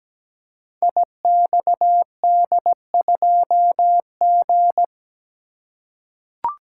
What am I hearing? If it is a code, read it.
Morse code, heard as a keyed tone, about 17 wpm, 700 Hz. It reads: IXD2G
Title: Code Groups - 5 Letter:Numbers - 17wpm